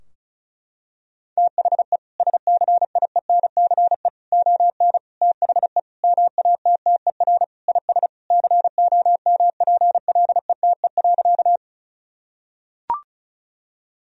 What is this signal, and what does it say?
Morse code, heard as a keyed tone, about 34 wpm, 700 Hz. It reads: THE SCIENCE ON THE MATTER IS COMPLETE.
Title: Sentences from Top 1000 Words - Review Part 2 - 35wpm